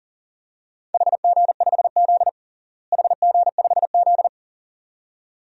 Morse code, heard as a keyed tone, about 40 wpm, 700 Hz. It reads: HG5Z HG5Z
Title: DXCC-100 Call Signs - 4 Characters - Repeated 2x - 40wpm